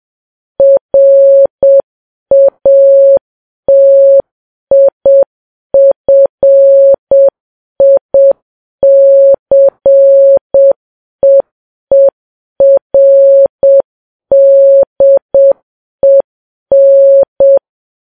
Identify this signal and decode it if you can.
Morse code, heard as a keyed tone, about 7 wpm, 550 Hz. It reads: RATIFICEERDEN
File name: Files Web tls / Morse on9fv / morse7kps